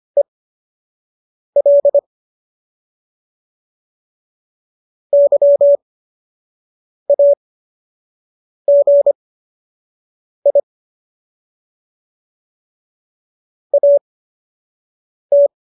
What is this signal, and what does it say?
Morse code, heard as a keyed tone, about 25 wpm, spaced as for 6 wpm, 570 Hz. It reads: EL YAGI AT